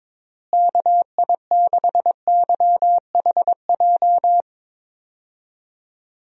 Morse code, heard as a keyed tone, about 22 wpm, 700 Hz. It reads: KI6YHJ
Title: USA Call Signs - 6 Characters - Repeated 2x - 22wpm